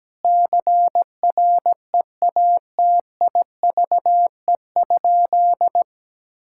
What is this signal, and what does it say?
Morse code, heard as a keyed tone, about 17 wpm, 700 Hz. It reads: CREATIVE?
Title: Sentences from Top 1000 Words - Review Part 5 - 17wpm